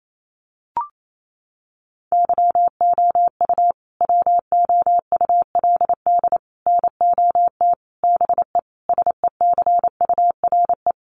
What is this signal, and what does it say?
Morse code, heard as a keyed tone, about 28 wpm, 700 Hz. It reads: YOU WOULD NOT BE SECURE
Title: Sentences from Top 1200 Words 28wpm